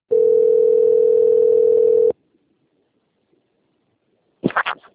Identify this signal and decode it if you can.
Ringback tone (North America), 440 + 480 Hz.